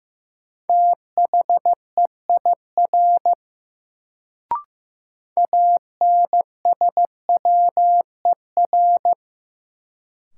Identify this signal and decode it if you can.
Morse code, heard as a keyed tone, about 15 wpm, 700 Hz. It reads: THEIR ANSWER
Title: Sentences from Top 300 Words - New Word Drill - 15wpm